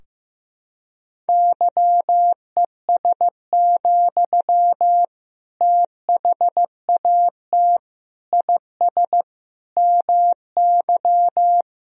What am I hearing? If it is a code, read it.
Morse code, heard as a keyed tone, about 15 wpm, 700 Hz. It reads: YES, THAT IS MY
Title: Sentences from Top 1000 Words - Review Part 3 - 15wpm